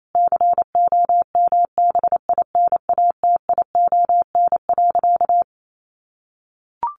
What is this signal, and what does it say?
Morse code, heard as a keyed tone, about 28 wpm, 700 Hz. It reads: COMBINATION.
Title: Sentences from Top 1300 Words 28wpm